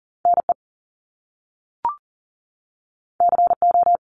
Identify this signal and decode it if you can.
Morse code, heard as a keyed tone, about 40 wpm, 700 Hz. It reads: NE CO
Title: Top Two Letter Combinations in Words 40wpm